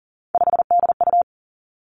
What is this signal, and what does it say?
Morse code, heard as a keyed tone, about 40 wpm, 700 Hz. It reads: 5DU
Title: Three Challenging Letter-Number Combinations 40wpm